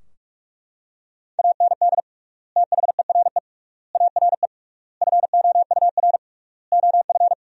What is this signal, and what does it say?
Morse code, heard as a keyed tone, about 42 wpm, 700 Hz. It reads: AND THERE ARE FOUR OF
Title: 3x Word-Spacing - Sentences from Top 300 Words - 45wpm